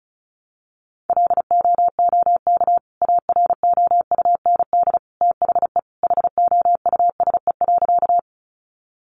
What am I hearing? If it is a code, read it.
Morse code, heard as a keyed tone, about 34 wpm, 700 Hz. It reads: LOOK AROUND THE HOUSE.
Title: Sentences from Top 600 Words 35wpm